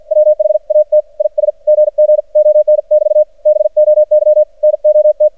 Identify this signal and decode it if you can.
Morse code, heard as a keyed tone, about 53 wpm, spaced as for 46 wpm, 610 Hz. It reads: WHAT IS GG ON<BT> BOY NOT
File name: archived / CRSnet